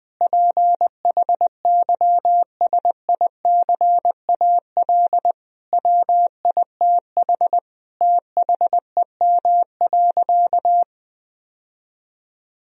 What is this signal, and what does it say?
Morse code, heard as a keyed tone, about 20 wpm, 700 Hz. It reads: PHYSICAL WITH THEM.
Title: Sentences from Top 1000 Words - Review Part 4 - 20wpm